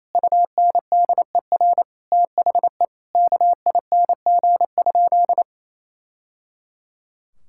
Morse code, heard as a keyed tone, about 28 wpm, 700 Hz. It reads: UNDER THE KING?
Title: Sentences from Top 1000 Words - Review Part 6 - 28wpm